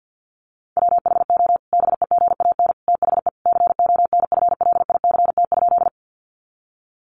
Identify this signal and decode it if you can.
Morse code, heard as a keyed tone, about 48 wpm, 700 Hz. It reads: WHO BEGAN THE CONFLICT?
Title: Sentences from Top 1300 Words 50wpm